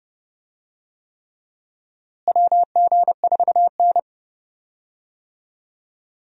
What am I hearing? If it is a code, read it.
Morse code, heard as a keyed tone, about 30 wpm, 700 Hz. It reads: WG4N
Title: USA Call Signs - 4 Characters - Repeated 3x - 30wpm